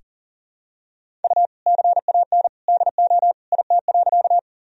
Morse code, heard as a keyed tone, about 40 wpm, 700 Hz. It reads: U CAN DO IT.
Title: Sentences from Top 100 Words and Abbreviations 40wpm